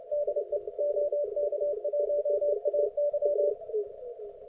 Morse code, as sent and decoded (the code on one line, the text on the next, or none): -.. / --- -.- .---- .- -.--
DE OK1AY